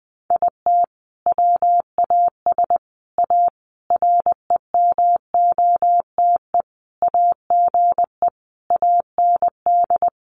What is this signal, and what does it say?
Morse code, heard as a keyed tone, about 20 wpm, 700 Hz. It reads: IT WAS A REMOTE AGE AND
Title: Sentences from Top 1300 Words 20wpm